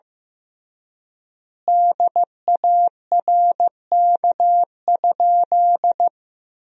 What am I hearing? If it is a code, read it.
Morse code, heard as a keyed tone, about 15 wpm, 700 Hz. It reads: DARK?
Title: 3x Word-Spacing - Sentences from Top 500 Words - 15wpm